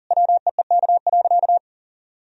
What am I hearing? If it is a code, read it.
Morse code, heard as a keyed tone, about 40 wpm, 700 Hz. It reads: WEEK.
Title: Sentences from Top 1000 Words - Review Part 2 - 40wpm